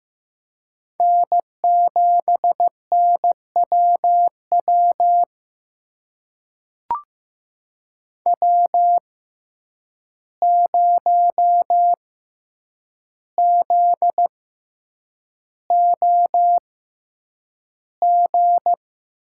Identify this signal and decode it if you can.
Morse code, heard as a keyed tone, about 15 wpm, spaced as for 8 wpm, 700 Hz. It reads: N7NWW W0ZOG
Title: USA Call Signs - 5 Characters - 6x Character Spacing - 15wpm